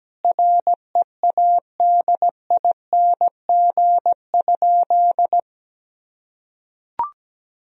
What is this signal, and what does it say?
Morse code, heard as a keyed tone, about 17 wpm, 700 Hz. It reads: READING?
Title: Sentences from Top 1000 Words - Review Part 3 - 17wpm